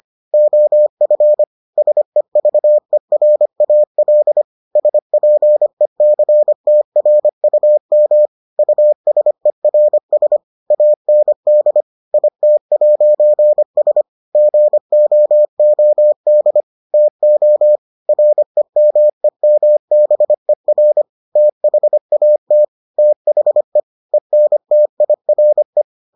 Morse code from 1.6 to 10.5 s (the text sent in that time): SEVERAL SPECTRUM USERS